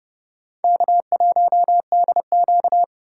Morse code, heard as a keyed tone, about 31 wpm, 700 Hz. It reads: K1DQ
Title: USA Call Signs - 4 Characters - 4x Character Spacing - 30wpm